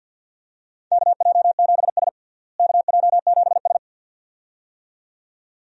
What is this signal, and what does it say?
Morse code, heard as a keyed tone, about 53 wpm, 700 Hz. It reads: KJ7S KJ7S
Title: USA Call Signs - 4 Characters - Repeated 2x - 50wpm